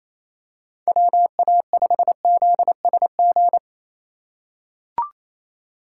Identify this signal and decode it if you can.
Morse code, heard as a keyed tone, about 28 wpm, 700 Hz. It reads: WA5ZSG
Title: USA Call Signs - 6 Characters - 2x Character Spacing - 28wpm